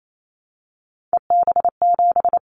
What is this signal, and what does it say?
Morse code, heard as a keyed tone, about 28 wpm, 700 Hz. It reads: EB7
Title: Callsign Prefixes for Top 25 DXCC - 28wpm